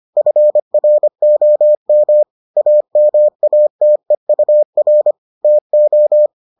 Morse code, heard as a keyed tone, about 25 wpm, 600 Hz. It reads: FROM AMATEUR TO